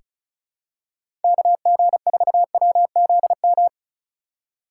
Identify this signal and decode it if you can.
Morse code, heard as a keyed tone, about 36 wpm, 690 Hz. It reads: KG4WZM
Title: USA Call Signs - 6 Characters - 6x Character Spacing - 35wpm